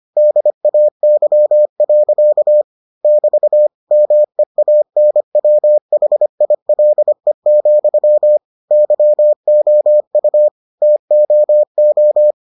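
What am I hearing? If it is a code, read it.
Morse code, heard as a keyed tone, about 25 wpm, 600 Hz. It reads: DAY. <BT> MEANWHILE, YOU TOO
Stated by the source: Morse edition of the Foundations of Amateur Radio episodes.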